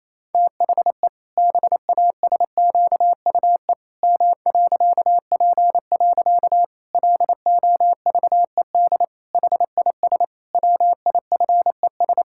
Morse code, heard as a keyed tone, about 28 wpm, 700 Hz. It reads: THE BASQUE M.P. LOVED HIS WIFES
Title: Pangram Challenge v2 - 28wpm